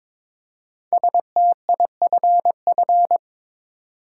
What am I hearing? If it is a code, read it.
Morse code, heard as a keyed tone, about 22 wpm, 700 Hz. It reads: STIFF
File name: 5 Letter Words - First 3 Letters Given - 22wpm